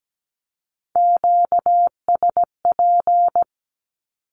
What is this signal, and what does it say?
Morse code, heard as a keyed tone, about 17 wpm, 700 Hz. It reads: QSP
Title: Q-Codes with Spoken Meaning - 17wpm